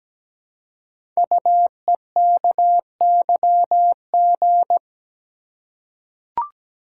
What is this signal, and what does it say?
Morse code, heard as a keyed tone, about 17 wpm, 700 Hz. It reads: UEKYG
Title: Code Groups - 5 Letters - 17wpm